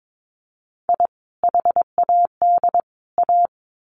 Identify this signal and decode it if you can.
Morse code, heard as a keyed tone, about 22 wpm, 700 Hz. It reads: I HAD A
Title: Sets of 3 Words 22wpm